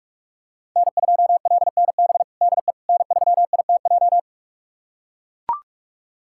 Morse code, heard as a keyed tone, about 41 wpm, 700 Hz. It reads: N2LNB DE N3ITJ
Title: USA Callsign DE Callsign - 5 Characters - 45wpm